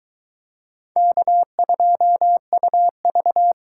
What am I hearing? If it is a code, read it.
Morse code, heard as a keyed tone, about 22 wpm, 700 Hz. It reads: K2UV